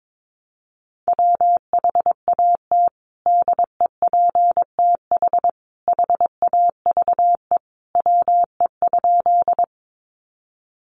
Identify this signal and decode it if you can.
Morse code, heard as a keyed tone, about 22 wpm, 700 Hz. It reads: WHAT DEPTH HAVE WE?
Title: Sentences from Top 1200 Words 22wpm